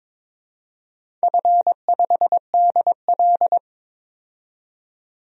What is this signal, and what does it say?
Morse code, heard as a keyed tone, about 23 wpm, 700 Hz. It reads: F5DL
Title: DXCC-100 Call Signs - 4 Characters - 22wpm